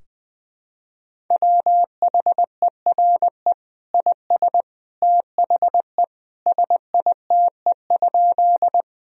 Morse code, heard as a keyed tone, about 20 wpm, 700 Hz. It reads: WHERE IS THE SITE?
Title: Sentences from Top 1000 Words - Review Part 4 - 20wpm